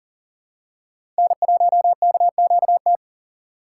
Morse code, heard as a keyed tone, about 40 wpm, 700 Hz. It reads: N1KQT
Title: USA Call Signs - 4x Character Spacing - 40wpm